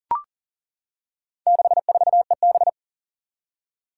Morse code, heard as a keyed tone, about 40 wpm, 700 Hz. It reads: 64EB
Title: Four Challenging Letter-Number Combinations 40wpm